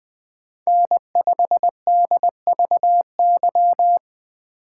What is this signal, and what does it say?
Morse code, heard as a keyed tone, about 20 wpm, 700 Hz. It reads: N5DVY
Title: USA Call Signs - 3x Character Spacing - 20wpm